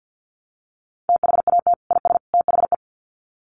Morse code, heard as a keyed tone, about 50 wpm, 700 Hz. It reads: THAT IS THE
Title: Sets of 3 Words 50wpm